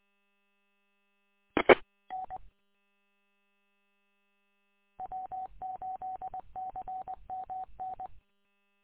Morse code, heard as a keyed tone, about 20 wpm, 750 Hz. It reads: I N W8CMN